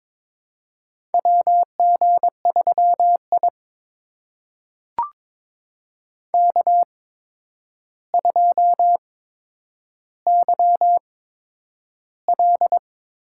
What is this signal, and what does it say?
Morse code, heard as a keyed tone, about 22 wpm, spaced as for 9 wpm, 700 Hz. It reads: WG3I K2YL
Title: USA Call Signs - 4 Characters - 8x Character Spacing - 22wpm